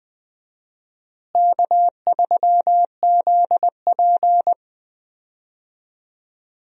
Morse code, heard as a keyed tone, about 20 wpm, 700 Hz. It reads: K3ZP
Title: USA Call Signs - 4 Characters - Repeated 2x - 20wpm